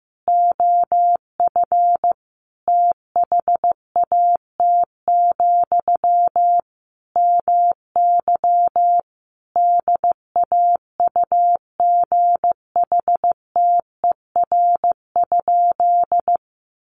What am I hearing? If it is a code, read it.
Morse code, heard as a keyed tone, about 15 wpm, 700 Hz. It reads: OF THAT, MY DAUGHTER?
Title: Sentences from Top 1500 Words 15wpm